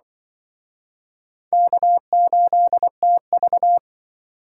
Morse code, heard as a keyed tone, about 27 wpm, 700 Hz. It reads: K8TV